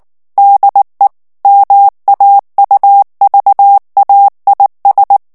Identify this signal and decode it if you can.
Morse code, heard as a keyed tone, about 19 wpm, 800 Hz. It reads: DE MAUVAIS
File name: FRANCAIS TEXTE CLAIR 1 VITESSE 18 MOTS MINUTES